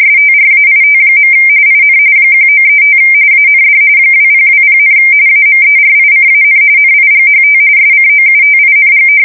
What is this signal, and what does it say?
RTTY, about 45.45 baud. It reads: STRAIT. IN 1935 AND 2020, AMERICAN NIGHT SCOUTING
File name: fsk170 / battleshipnj